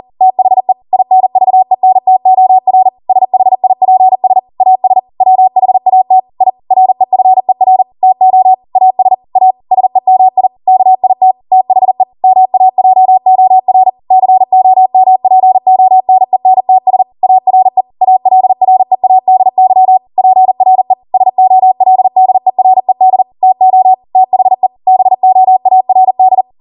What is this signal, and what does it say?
Morse code, heard as a keyed tone, about 40 wpm, 750 Hz. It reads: THE INVENTOR SHIPS AS WHAT I REFER TO AS A SEMI KIT THE MAJOR COMPONENTS ARE ALREADY PRE SOLDERED TO THE BOARD